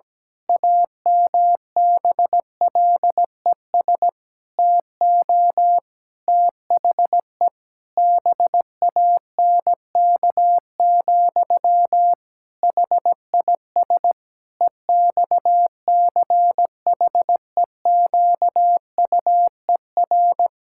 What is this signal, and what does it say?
Morse code, heard as a keyed tone, about 17 wpm, 700 Hz. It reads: AMBLES TO THE BANK, HIS EXCHEQUER
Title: Pangram Challenge v2 - 17wpm